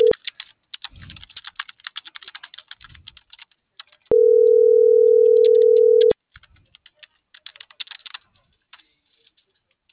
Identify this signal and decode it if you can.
Ringback tone (North America), 440 + 480 Hz.